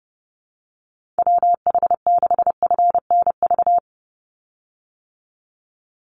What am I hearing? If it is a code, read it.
Morse code, heard as a keyed tone, about 30 wpm, 700 Hz. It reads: WH6FNV
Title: USA Call Signs 30wpm